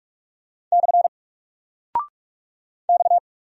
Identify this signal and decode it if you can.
Morse code, heard as a keyed tone, about 46 wpm, 700 Hz. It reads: / X
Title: Single Character (Letter:Number:Punctuation:Prosign) 45wpm